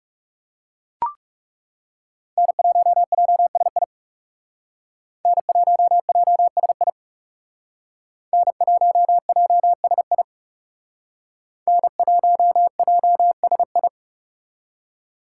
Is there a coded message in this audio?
Yes: Morse code from 2.4 to 13.9 s, heard as a keyed tone, about 39 wpm, 700 Hz. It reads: N1JSI N1JSI N1JSI N1JSI